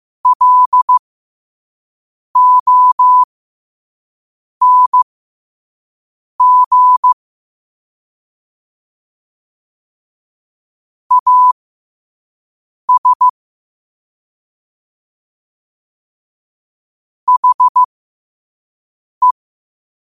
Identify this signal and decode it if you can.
Morse code, heard as a keyed tone, about 15 wpm, spaced as for 5 wpm, 1000 Hz. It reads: LONG AS HE
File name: qotdfiles / 5wpm